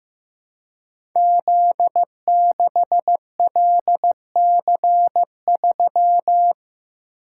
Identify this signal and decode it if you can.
Morse code, heard as a keyed tone, about 15 wpm, 700 Hz. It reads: Z6LC3
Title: Code Groups - 5 Letter:Numbers - 15wpm